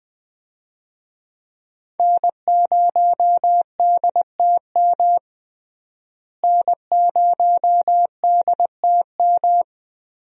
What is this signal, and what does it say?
Morse code, heard as a keyed tone, about 20 wpm, 700 Hz. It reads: N0DTM N0DTM
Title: USA Call Signs - 5 Characters - Repeated 3x - 20wpm